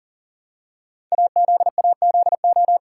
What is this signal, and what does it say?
Morse code, heard as a keyed tone, about 40 wpm, 690 Hz. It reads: AZAZO
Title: Code Groups - 5 Letters - 40wpm